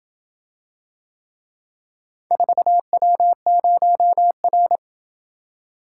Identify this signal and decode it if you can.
Morse code, heard as a keyed tone, about 27 wpm, 700 Hz. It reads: 4W0R